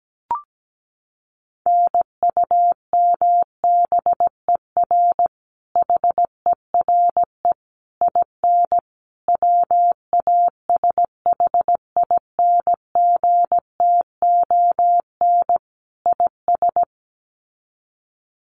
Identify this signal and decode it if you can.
Morse code, heard as a keyed tone, about 17 wpm, 700 Hz. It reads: NUMBER HERE IN WASHINGTON IS
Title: Sets of 5 Words Encore2 17wpm